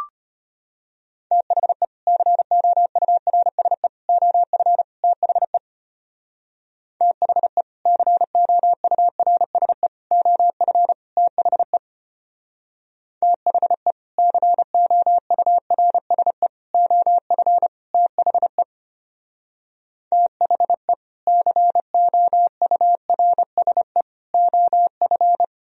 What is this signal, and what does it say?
Morse code, heard as a keyed tone, about 31 wpm, 700 Hz. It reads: THE COURSE OF THE THE COURSE OF THE THE COURSE OF THE THE COURSE OF